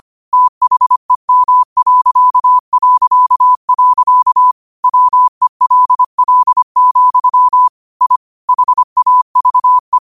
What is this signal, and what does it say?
Morse code, heard as a keyed tone, about 25 wpm, 1000 Hz. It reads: THEM... WELL, I HAVE
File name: qotdfiles / 25wpm